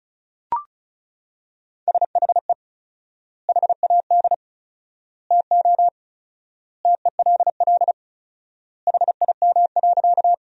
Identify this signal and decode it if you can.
Morse code, heard as a keyed tone, about 36 wpm, 700 Hz. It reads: SHE HAD TO TELL HIM.
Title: 4x Word-Spacing - Sentences from Top 200 Words - 35wpm